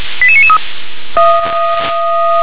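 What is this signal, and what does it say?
Five-tone selective call (ZVEI), heard as quick runs of five tones, tones of about 70 ms. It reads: 88993
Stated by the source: ZVEI five-tone selective call received on SDR.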